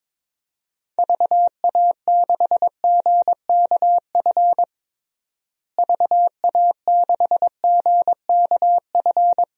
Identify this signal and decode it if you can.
Morse code, heard as a keyed tone, about 22 wpm, 700 Hz. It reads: VA6GKF VA6GKF
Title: DXCC-100 Call Signs - 6 Characters - Repeated 3x - 22wpm